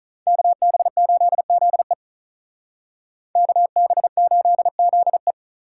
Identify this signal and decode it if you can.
Morse code, heard as a keyed tone, about 43 wpm, 700 Hz. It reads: KB8ZE KB8ZE
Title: Speed Racing - USA 2x2 Call Signs 35wpm